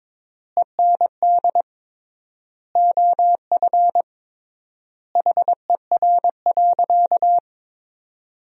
Morse code, heard as a keyed tone, about 22 wpm, 700 Hz. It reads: END OF HER.
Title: 3x Word-Spacing - Sentences from Top 200 Words - 22wpm